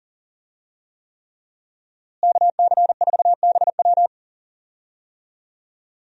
Morse code, heard as a keyed tone, about 40 wpm, 700 Hz. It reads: KC4BW
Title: USA Call Signs - 5 Characters - Repeated 3x - 40wpm